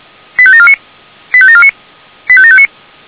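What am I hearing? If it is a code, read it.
Five-tone selective call (ZVEI), heard as quick runs of five tones, tones of about 70 ms. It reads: 85749 85749 85759
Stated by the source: ZVEI five-tone selective call 85729 received by SDR.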